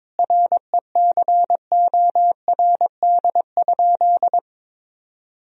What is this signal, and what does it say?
Morse code, heard as a keyed tone, about 22 wpm, 700 Hz. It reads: RECORD?
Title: Sentences from Top 1000 Words - Review Part 3 - 22wpm